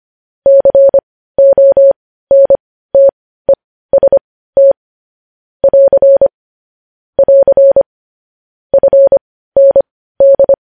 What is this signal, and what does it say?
Morse code, heard as a keyed tone, about 25 wpm, spaced as for 15 wpm, 550 Hz. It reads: CONTEST <AR> <AR> FND
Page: news headlines in morse code at 15 wpm